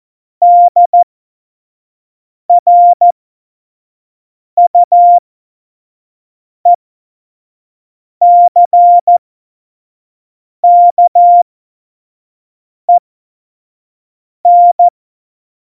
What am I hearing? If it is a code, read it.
Morse code, heard as a keyed tone, about 14 wpm, spaced as for 5 wpm, 700 Hz. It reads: DRUECKEN
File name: Morsekurs / Witze / Witz 06. Februar 2018, 25 F70